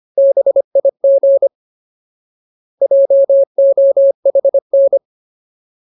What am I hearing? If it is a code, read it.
Morse code, heard as a keyed tone, about 25 wpm, 550 Hz. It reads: BIG JOHN
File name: Ultimate Word List / 25WPM Ultimate